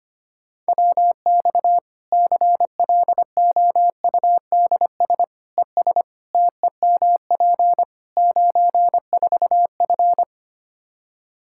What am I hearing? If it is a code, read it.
Morse code, heard as a keyed tone, about 25 wpm, 700 Hz. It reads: WX CLOUDS ES TEMP 94F
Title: Practice QSO - Exchange 2 - Info - 25wpm